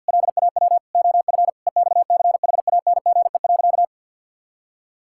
Morse code, heard as a keyed tone, about 52 wpm, 700 Hz. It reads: LAW OF EXCHANGE.
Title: Sentences from Top 1000 Words - Review Part 6 - 50wpm